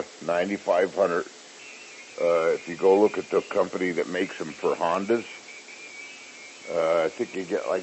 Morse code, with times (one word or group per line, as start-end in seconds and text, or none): none